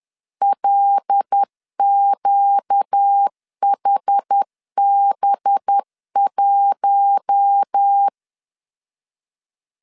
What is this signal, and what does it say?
Morse code, heard as a keyed tone, about 10 wpm, 800 Hz. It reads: LQHB1
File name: CW / DET / 1 6 1500C S B 10wpm 4137